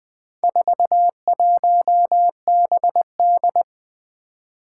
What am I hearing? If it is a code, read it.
Morse code, heard as a keyed tone, about 20 wpm, 700 Hz. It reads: 41BD
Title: Four Challenging Letter-Number Combinations 20wpm